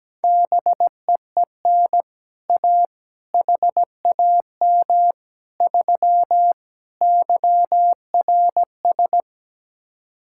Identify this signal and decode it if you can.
Morse code, heard as a keyed tone, about 17 wpm, 700 Hz. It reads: BEEN A HAM 3 YRS
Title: Speed-Racing Common QSO Phrases - 17wpm